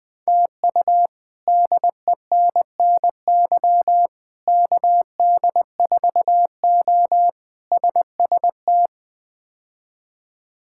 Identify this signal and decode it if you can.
Morse code, heard as a keyed tone, about 21 wpm, 700 Hz. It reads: TU DENNY KD4O SST
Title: K1USN SST QSO Practice - 20wpm